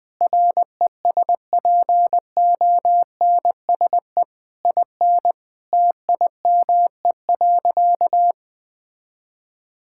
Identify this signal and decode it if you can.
Morse code, heard as a keyed tone, about 20 wpm, 700 Hz. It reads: RESPONSE IN TIME.
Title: Sentences with 8-Word Phrases - Spoken Beginning - 20wpm